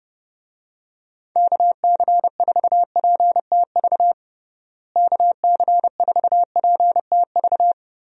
Morse code, heard as a keyed tone, about 30 wpm, 700 Hz. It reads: KC4PTV KC4PTV
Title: USA Call Signs - 6 Characters - Repeated 2x - 30wpm